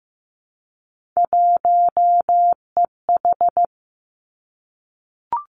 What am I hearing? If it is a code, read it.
Morse code, heard as a keyed tone, about 15 wpm, 700 Hz. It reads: 1EH
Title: Three Challenging Letter-Number Combinations 15wpm